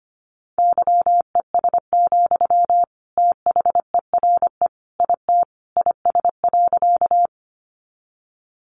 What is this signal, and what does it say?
Morse code, heard as a keyed tone, about 25 wpm, 700 Hz. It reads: YES, THERE IT IS.
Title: Sentences from Top 500 Words 25wpm